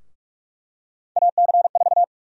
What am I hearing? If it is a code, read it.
Morse code, heard as a keyed tone, about 46 wpm, 700 Hz. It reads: AC4
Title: USA Call Sign Prefixes - 45wpm